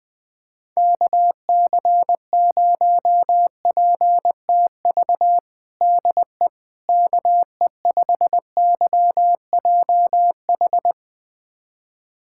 Morse code, heard as a keyed tone, about 21 wpm, 700 Hz. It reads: KC0PTV DE KE5YJH
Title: USA Callsign DE Callsign 20wpm